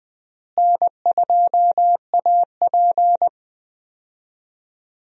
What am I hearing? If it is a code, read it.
Morse code, heard as a keyed tone, about 20 wpm, 700 Hz. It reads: N2AP
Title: USA Call Signs - 4 Characters - Repeated 3x - 20wpm